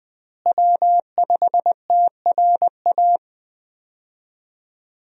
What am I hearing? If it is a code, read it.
Morse code, heard as a keyed tone, about 20 wpm, 700 Hz. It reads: W5TRA
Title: USA Call Signs - 5 Characters - Repeated 2x - 20wpm